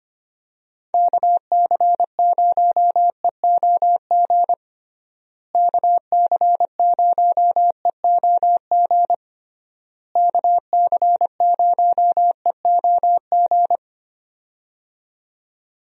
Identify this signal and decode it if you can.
Morse code, heard as a keyed tone, about 25 wpm, 700 Hz. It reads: KC0EOG KC0EOG KC0EOG
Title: USA Call Signs - 6 Characters - Repeated 3x - 25wpm